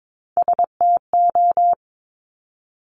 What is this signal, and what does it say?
Morse code, heard as a keyed tone, about 22 wpm, 700 Hz. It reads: STO
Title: Top Three Letter Combinations in Words 22wpm